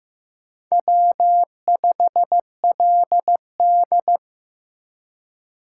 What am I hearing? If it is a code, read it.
Morse code, heard as a keyed tone, about 15 wpm, 700 Hz. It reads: W5LD
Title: USA Call Signs - 4 Characters - 3x Character Spacing - 15wpm